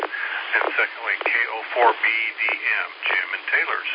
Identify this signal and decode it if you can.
Emergency Alert System (SAME) tones, heard as harsh buzzing bursts, with speech. It is heard from 0.8 to 3.7 s.